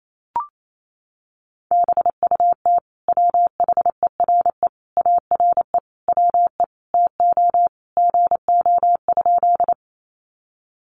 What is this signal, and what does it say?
Morse code, heard as a keyed tone, about 28 wpm, 700 Hz. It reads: BUT WHERE ARE WE TO GO?
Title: Sentences from Top 200 Words 28wpm